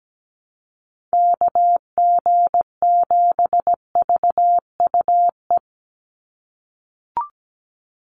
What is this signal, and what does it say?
Morse code, heard as a keyed tone, about 17 wpm, 700 Hz. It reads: KG7VUE
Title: USA Call Signs 17wpm